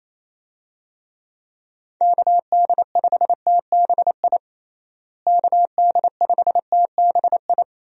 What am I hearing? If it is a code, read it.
Morse code, heard as a keyed tone, about 28 wpm, 700 Hz. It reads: KD5TBI KD5TBI
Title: USA Call Signs - 6 Characters - Repeated 2x - 28wpm